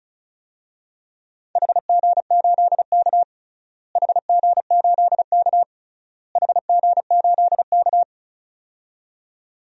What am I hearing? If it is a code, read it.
Morse code, heard as a keyed tone, about 36 wpm, 700 Hz. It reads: HG8K HG8K HG8K
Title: DXCC-100 Call Signs - 4 Characters - Repeated 3x - 35wpm